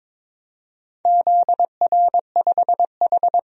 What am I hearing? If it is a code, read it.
Morse code, heard as a keyed tone, about 22 wpm, 700 Hz. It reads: ZR5H
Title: DXCC-100 Call Signs - 4 Characters - Repeated 2x - 22wpm